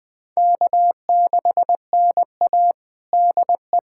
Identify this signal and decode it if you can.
Morse code, heard as a keyed tone, about 21 wpm, 700 Hz. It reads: K6NA DE
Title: USA Callsign DE Callsign - 4 Characters - 20wpm